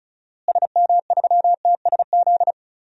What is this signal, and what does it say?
Morse code, heard as a keyed tone, about 36 wpm, 700 Hz. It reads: SM3TSZ
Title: DXCC-100 Call Signs - 6 Characters - Repeated 2x - 35wpm